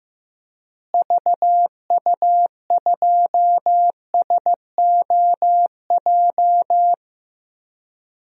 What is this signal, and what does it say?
Morse code, heard as a keyed tone, about 15 wpm, 700 Hz. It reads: VU2SOJ
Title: DXCC-100 Call Signs - 6 Characters - Repeated 2x - 15wpm